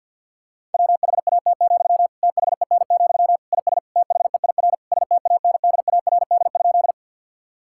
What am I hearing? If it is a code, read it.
Morse code, heard as a keyed tone, about 50 wpm, 700 Hz. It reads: WHAT, THEN, IS THEIR STANDARD?
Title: Sentences from Top 1000 Words - Review Part 6 - 50wpm